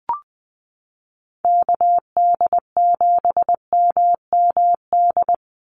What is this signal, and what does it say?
Morse code, heard as a keyed tone, about 20 wpm, 700 Hz. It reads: KD7MMD
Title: USA Call Signs 20wpm